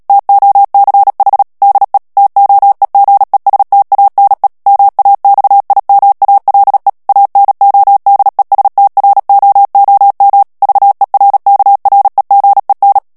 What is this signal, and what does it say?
Morse code, heard as a keyed tone, about 37 wpm, 800 Hz. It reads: TOCH DE TOEGESTANE MAXIMALE ANODESTROOM VERKREGEN
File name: morse mp3 / HOLLANDAIS texte clair 2 vitesse 35 mots minute